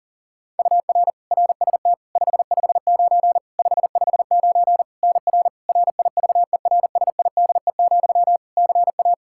Morse code, heard as a keyed tone, about 40 wpm, 700 Hz. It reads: UR RST 559 559 NR RIVERSIDE, CA
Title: Practice QSO - Exchange 1 - Beginning - 40wpm